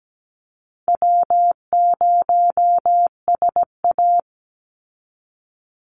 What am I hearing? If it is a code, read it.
Morse code, heard as a keyed tone, about 17 wpm, 700 Hz. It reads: W0SA
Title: Easy Call Signs - 17wpm